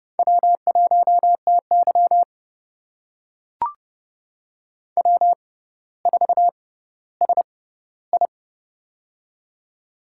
Morse code, heard as a keyed tone, about 30 wpm, spaced as for 15 wpm, 700 Hz. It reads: W1TY W4SI
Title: USA Call Signs - 4 Characters - 6x Character Spacing - 30wpm